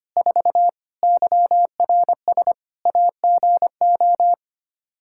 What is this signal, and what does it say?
Morse code, heard as a keyed tone, about 25 wpm, 700 Hz. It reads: 4 YRS AGO
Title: Practice QSO - Exchange 3 - More Info - 25wpm